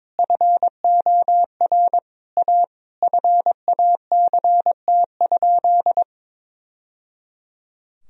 Morse code, heard as a keyed tone, about 22 wpm, 700 Hz. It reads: FOR A FACT?
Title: Sentences from Top 1000 Words - Review Part 3 - 22wpm